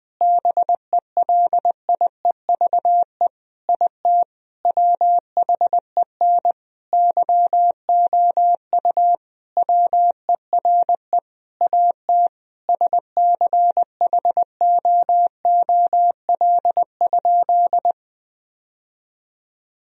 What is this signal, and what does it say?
Morse code, heard as a keyed tone, about 21 wpm, 700 Hz. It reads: BELIEVE IT WHEN YOU WERE AT SCHOOL?
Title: Sentences from Top 1000 Words - Review Part 1 - 20wpm